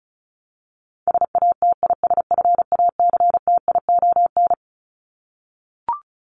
Morse code, heard as a keyed tone, about 34 wpm, 700 Hz. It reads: SATISFACTION
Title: Top 1751-2000 Words 35wpm